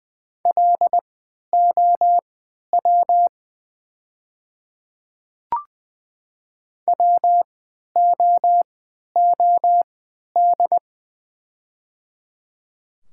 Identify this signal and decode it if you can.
Morse code, heard as a keyed tone, about 20 wpm, spaced as for 11 wpm, 700 Hz. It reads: LOW WOOD
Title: 3x Character Spacing - Top 300 Words - 20wpm